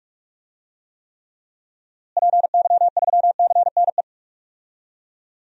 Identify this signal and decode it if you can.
Morse code, heard as a keyed tone, about 43 wpm, 700 Hz. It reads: PY3CNE
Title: DXCC-100 Call Signs - 6 Characters - Repeated 3x - 45wpm